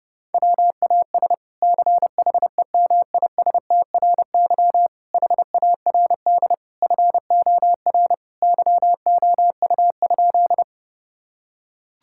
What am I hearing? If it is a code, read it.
Morse code, heard as a keyed tone, about 30 wpm, 700 Hz. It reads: WAS CHEMISTRY HARD FOR YOU?
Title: Sentences from Top 1000 Words - Review Part 6 - 30wpm